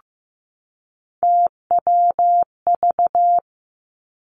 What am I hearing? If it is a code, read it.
Morse code, heard as a keyed tone, about 15 wpm, 700 Hz. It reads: TWV
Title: Three Challenging Letter-Number Combinations 15wpm